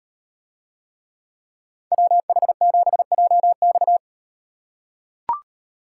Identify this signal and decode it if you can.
Morse code, heard as a keyed tone, about 38 wpm, 700 Hz. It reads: WH7JX